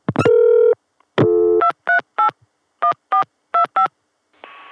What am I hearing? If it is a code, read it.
DTMF, heard as telephone touch tones, tones of about 100 ms. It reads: 238 1425